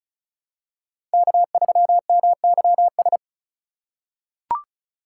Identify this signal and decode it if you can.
Morse code, heard as a keyed tone, about 36 wpm, 700 Hz. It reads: K3MYS
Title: USA Call Signs - 5 Characters - 4x Character Spacing - 35wpm